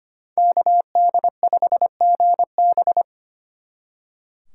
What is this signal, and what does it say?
Morse code, heard as a keyed tone, about 33 wpm, 700 Hz. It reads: KD5GB